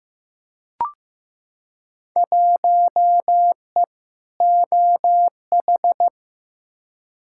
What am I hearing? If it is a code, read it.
Morse code, heard as a keyed tone, about 15 wpm, 700 Hz. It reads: 1E OH
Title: Field Day Exchanges - 15wpm